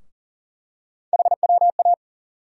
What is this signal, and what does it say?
Morse code, heard as a keyed tone, about 40 wpm, 700 Hz. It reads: HWA
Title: USA Call Sign Postfixes - 3 Letters 40wpm